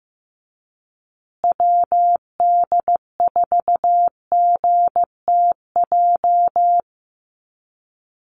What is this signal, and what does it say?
Morse code, heard as a keyed tone, about 15 wpm, 700 Hz. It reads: WD4GTJ
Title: USA Call Signs Repeated 2x 15wpm